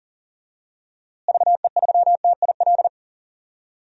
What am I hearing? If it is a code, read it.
Morse code, heard as a keyed tone, about 38 wpm, 700 Hz. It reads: VE3TIL
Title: DXCC-100 Call Signs - 6 Characters - Repeated 3x - 40wpm